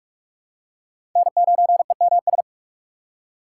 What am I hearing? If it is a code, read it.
Morse code, heard as a keyed tone, about 46 wpm, 700 Hz. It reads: N9EMS
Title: USA Call Signs - 5 Characters - 6x Character Spacing - 45wpm